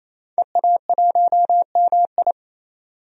Morse code, heard as a keyed tone, about 27 wpm, 700 Hz. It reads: EA1MI
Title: DXCC-100 Call Signs - 5 Characters - 28wpm